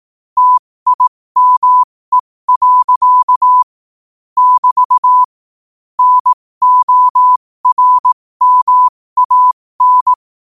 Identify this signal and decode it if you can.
Morse code, heard as a keyed tone, about 18 wpm, spaced as for 15 wpm, 1000 Hz. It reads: TIME. <BT> NORMAN
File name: qotdfiles / 15wpm